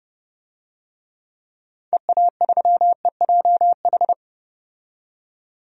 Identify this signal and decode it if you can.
Morse code, heard as a keyed tone, about 30 wpm, 700 Hz. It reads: EA3EJH
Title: DXCC-100 Call Signs - 6 Characters - 30wpm